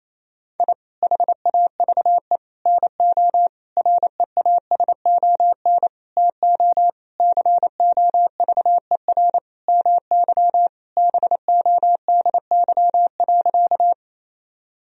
Morse code, heard as a keyed tone, about 28 wpm, 700 Hz. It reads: I HAVE NO REASON TO COVER MY BODY.
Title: Sentences from Top 1000 Words - Review Part 1 - 28wpm